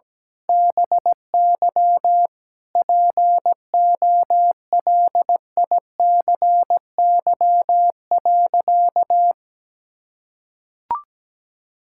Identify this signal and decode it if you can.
Morse code, heard as a keyed tone, about 17 wpm, 700 Hz. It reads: BY POLICY.
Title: Sentences from Top 1000 Words - Review Part 5 - 17wpm